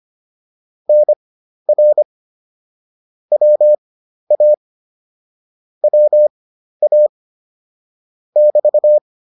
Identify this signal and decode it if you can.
Morse code, heard as a keyed tone, about 25 wpm, spaced as for 12 wpm, 600 Hz. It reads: NR WA WA <BT>